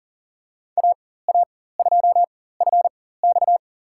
Morse code, heard as a keyed tone, about 38 wpm, spaced as for 18 wpm, 700 Hz. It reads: AA2FX
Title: USA Call Signs - 5 Characters - 4x Character Spacing - 40wpm